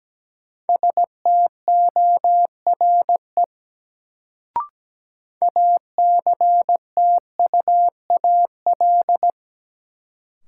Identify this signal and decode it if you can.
Morse code, heard as a keyed tone, about 17 wpm, 700 Hz. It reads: STORE ACTUAL
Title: Sentences from Top 1000 Words - Review Part 5 - New Word Drill - 17wpm